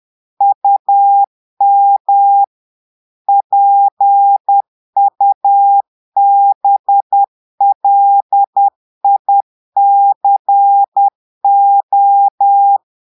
Morse code, heard as a keyed tone, about 10 wpm, 800 Hz. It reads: UM PUBLICO
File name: portugais texte en clair 2 vitesse 10 mots minutes